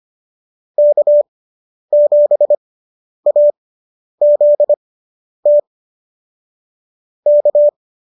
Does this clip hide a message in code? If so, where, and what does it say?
Yes: Morse code from 0.8 to 7.7 s, heard as a keyed tone, about 25 wpm, spaced as for 10 wpm, 590 Hz. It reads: K7AZT K